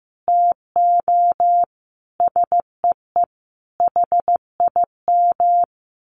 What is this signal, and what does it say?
Morse code, heard as a keyed tone, about 15 wpm, 700 Hz. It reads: TO SEE HIM
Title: Sentences from Top 1400 Words 15wpm